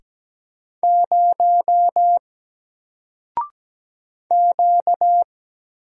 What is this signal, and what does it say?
Morse code, heard as a keyed tone, about 17 wpm, 700 Hz. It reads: 0 Q
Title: Single Character (Letter:Number:Punctuation:Prosign) Alt 17wpm